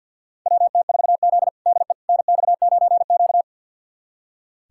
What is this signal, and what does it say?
Morse code, heard as a keyed tone, about 46 wpm, 700 Hz. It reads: WT4Z DE NX9Q
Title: USA Callsign DE Callsign - 4 Characters - 50wpm